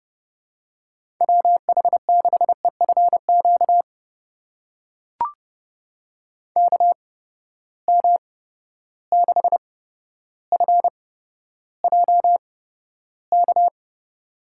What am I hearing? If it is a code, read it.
Morse code, heard as a keyed tone, about 30 wpm, spaced as for 13 wpm, 700 Hz. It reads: WH6EFQ KM6FJK
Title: USA Call Signs - 6 Characters - 8x Character Spacing - 30wpm